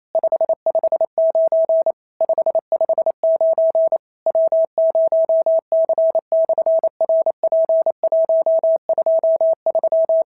Morse code, heard as a keyed tone, about 27 wpm, 650 Hz. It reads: 559 559 W0C/RP123
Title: SOTA QSO Practice 28wpm